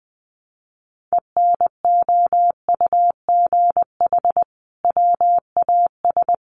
Morse code, heard as a keyed tone, about 20 wpm, 700 Hz. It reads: ENOUGH WAS
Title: Sentences from Top 1400 Words 20wpm